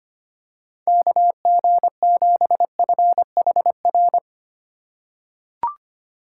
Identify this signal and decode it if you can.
Morse code, heard as a keyed tone, about 25 wpm, 700 Hz. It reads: KG7FHR
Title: USA Call Signs - 3x Character Spacing - 25wpm